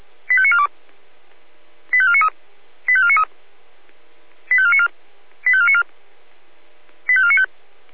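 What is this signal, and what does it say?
Five-tone selective call (ZVEI), heard as quick runs of five tones, tones of about 70 ms. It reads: 85942 85493 85493 85494 85494 85495
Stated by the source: Five-tone ZVEI selective call 85491 received by SDR.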